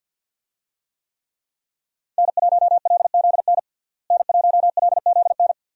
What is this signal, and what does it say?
Morse code, heard as a keyed tone, about 51 wpm, 700 Hz. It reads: N1LZN N1LZN
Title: USA Call Signs - 5 Characters - Repeated 3x - 50wpm